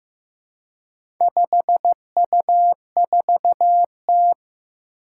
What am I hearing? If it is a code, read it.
Morse code, heard as a keyed tone, about 15 wpm, 700 Hz. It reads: 5U4T
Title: Four Challenging Letter-Number Combinations 15wpm